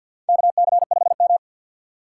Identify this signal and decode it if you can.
Morse code, heard as a keyed tone, about 49 wpm, 700 Hz. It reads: KC5M
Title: USA Call Signs - 4 Characters - 2x Character Spacing - 50wpm